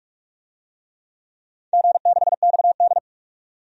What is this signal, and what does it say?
Morse code, heard as a keyed tone, about 44 wpm, 700 Hz. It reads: G6XD
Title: DXCC-100 Call Signs - 4 Characters - 45wpm